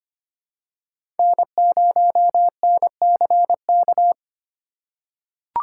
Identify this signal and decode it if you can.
Morse code, heard as a keyed tone, about 25 wpm, 700 Hz. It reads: N0NCK
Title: USA Call Signs - 5 Characters - 3x Character Spacing - 25wpm